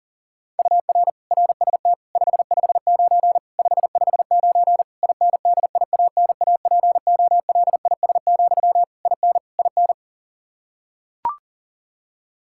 Morse code, heard as a keyed tone, about 40 wpm, 700 Hz. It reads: UR RST 559 559 INDIANAPOLIS, IN IN
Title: Practice QSO - Exchange 1 - Beginning - 40wpm